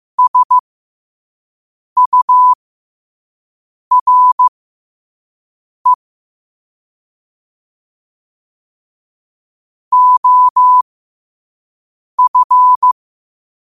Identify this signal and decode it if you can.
Morse code, heard as a keyed tone, about 15 wpm, spaced as for 5 wpm, 1000 Hz. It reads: SURE OF
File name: qotdfiles / 5wpm